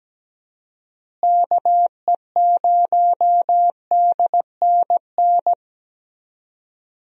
Morse code, heard as a keyed tone, about 17 wpm, 700 Hz. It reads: KE0DNN
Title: USA Call Signs - 6 Characters - Repeated 2x - 17wpm